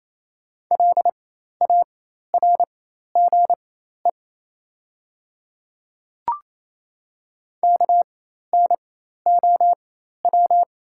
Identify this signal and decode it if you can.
Morse code, heard as a keyed tone, about 28 wpm, spaced as for 13 wpm, 700 Hz. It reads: LARGE KNOW
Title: 4x Character Spacing - Top 300 Words - 28wpm